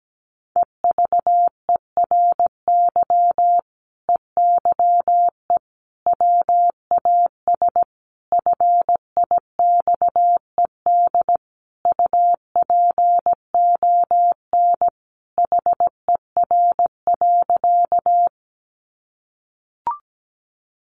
Morse code, heard as a keyed tone, about 17 wpm, 700 Hz. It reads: EVERY EYE WAS FIXED UPON HER.
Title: Sentences from Top 1100 Words 17wpm